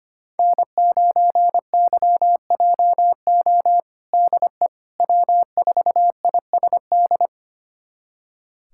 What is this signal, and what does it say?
Morse code, heard as a keyed tone, about 25 wpm, 700 Hz. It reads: N9YJO DE W4ISD
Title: USA Callsign DE Callsign - 5 Characters - 25wpm